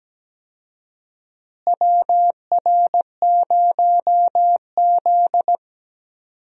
Morse code, heard as a keyed tone, about 17 wpm, 700 Hz. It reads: WR0Z
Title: USA Call Signs - 4 Characters - Repeated 2x - 17wpm